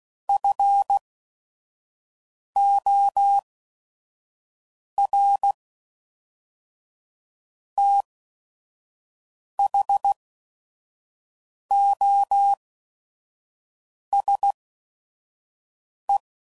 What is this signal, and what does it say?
Morse code, heard as a keyed tone, about 16 wpm, spaced as for 5 wpm, 800 Hz. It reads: FOR THOSE
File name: CW Code Groups @5WPM / Technical Subjects / AND Gate